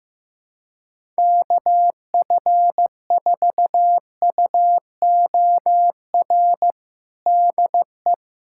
Morse code, heard as a keyed tone, about 15 wpm, 700 Hz. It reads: KF4UOR DE
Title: USA Callsign DE Callsign - 6 Characters - 15wpm